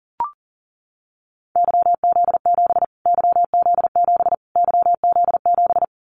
Morse code, heard as a keyed tone, about 40 wpm, 700 Hz. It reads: YZ7 YZ7 YZ7
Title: Callsign Prefixes for Top 25 DXCC - Repeated 3x - 40wpm